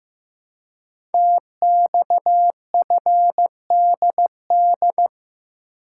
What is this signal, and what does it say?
Morse code, heard as a keyed tone, about 15 wpm, 700 Hz. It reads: TXFDD
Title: Code Groups - 5 Letters - 15wpm